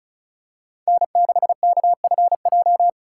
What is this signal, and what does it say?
Morse code, heard as a keyed tone, about 36 wpm, 690 Hz. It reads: N6KFJ
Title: USA Call Signs - 3x Character Spacing - 35wpm